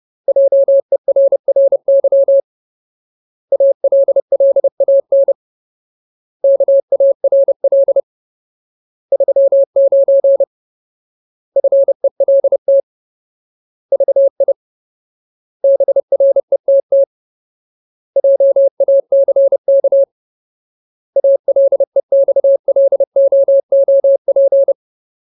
Morse code, heard as a keyed tone, about 30 wpm, 550 Hz. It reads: JERRY ALLAN KARL 39 FELT VI BRETT JACK ALEXLOOP
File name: Ultimate Word List / 30WPM Ultimate